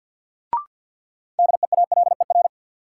Morse code, heard as a keyed tone, about 52 wpm, 700 Hz. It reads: DEALER
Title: Sentences from Top 1100 Words - New Word Drill Alt - 50wpm